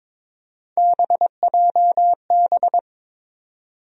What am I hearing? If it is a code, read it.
Morse code, heard as a keyed tone, about 22 wpm, 700 Hz. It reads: BJB
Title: USA Call Sign Postfixes - 3 Letters 22wpm